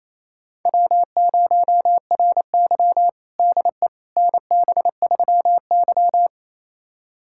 Morse code, heard as a keyed tone, about 28 wpm, 700 Hz. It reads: W0RY DE NB3Y
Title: USA Callsign DE Callsign - 4 Characters - 28wpm